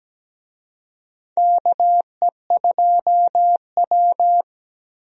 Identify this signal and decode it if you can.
Morse code, heard as a keyed tone, about 17 wpm, 700 Hz. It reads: KE2W
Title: USA Call Signs - 4 Characters - 17wpm